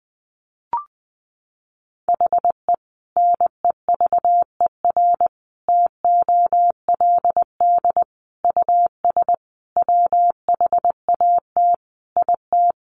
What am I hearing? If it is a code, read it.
Morse code, heard as a keyed tone, about 20 wpm, 700 Hz. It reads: HE NEVER TOLD US WHAT IT
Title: Sentences from Top 400 Words 20wpm